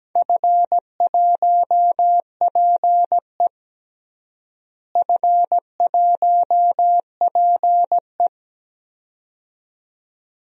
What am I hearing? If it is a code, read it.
Morse code, heard as a keyed tone, about 17 wpm, 700 Hz. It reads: F1PE F1PE
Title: DXCC-100 Call Signs - 4 Characters - Repeated 2x - 17wpm